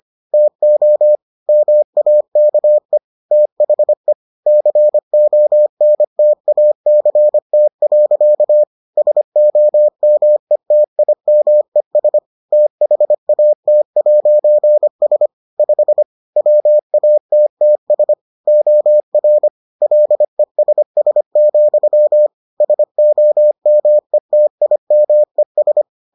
Morse code, heard as a keyed tone, about 25 wpm, 600 Hz. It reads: TO MAKE THE CONTACT. SOMETIMES THAT'S 5 WATTS OR LESS, SOMETIMES